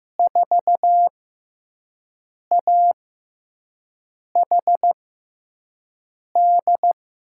Morse code, heard as a keyed tone, about 15 wpm, spaced as for 5 wpm, 700 Hz. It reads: 4AHD
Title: USA Call Signs - 6 Characters - 6x Character Spacing - 15wpm